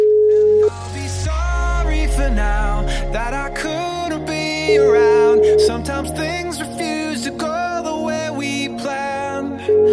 Ringback tone (Europe, 425 Hz), with noise.